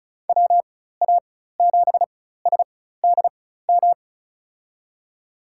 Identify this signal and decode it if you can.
Morse code, heard as a keyed tone, about 36 wpm, spaced as for 16 wpm, 700 Hz. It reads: WA7SDM
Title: USA Call Signs - 4x Character Spacing - 35wpm